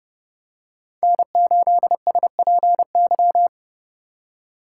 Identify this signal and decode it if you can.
Morse code, heard as a keyed tone, about 28 wpm, 700 Hz. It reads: N8SPY
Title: USA Call Signs - 5 Characters - 2x Character Spacing - 30wpm